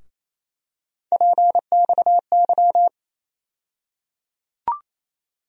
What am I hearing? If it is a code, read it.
Morse code, heard as a keyed tone, about 28 wpm, 700 Hz. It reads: PXY
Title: USA Call Sign Postfixes - 3 Letters 28wpm